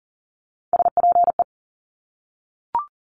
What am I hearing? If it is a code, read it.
Morse code, heard as a keyed tone, about 38 wpm, 700 Hz. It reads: SPE
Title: Top Three Letter Combinations in Words 40wpm